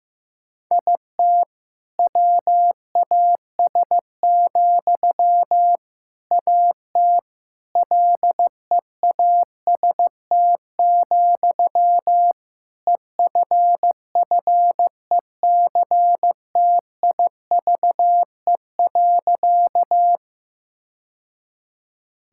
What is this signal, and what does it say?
Morse code, heard as a keyed tone, about 16 wpm, 700 Hz. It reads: IT WAS, AT LEAST, EFFECTIVE.
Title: Sentences from Top 1000 Words - Review Part 6 - 15wpm